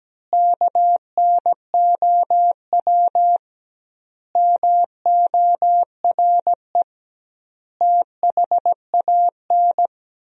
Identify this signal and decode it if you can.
Morse code, heard as a keyed tone, about 17 wpm, 700 Hz. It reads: KNOW MORE THAN
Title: 2x Word-Spacing - Sentences from Top 100 Words - 17wpm